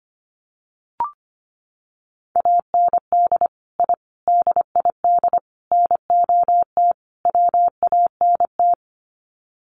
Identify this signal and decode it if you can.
Morse code, heard as a keyed tone, about 25 wpm, 700 Hz. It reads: AND I DID NOT WANT
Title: Sets of 5 Words 25wpm